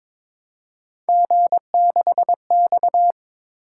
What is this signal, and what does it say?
Morse code, heard as a keyed tone, about 21 wpm, 700 Hz. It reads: G6X
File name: practice / 22wpm / Three Letter-Number Combinations - 22wpm